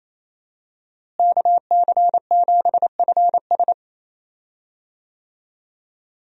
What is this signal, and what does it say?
Morse code, heard as a keyed tone, about 28 wpm, 700 Hz. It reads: KC7FS
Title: USA Call Signs - 5 Characters - Repeated 3x - 28wpm